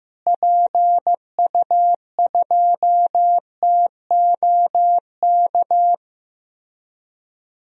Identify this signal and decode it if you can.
Morse code, heard as a keyed tone, about 15 wpm, 700 Hz. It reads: PU2TOK
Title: DXCC-100 Call Signs - 6 Characters - Repeated 3x - 15wpm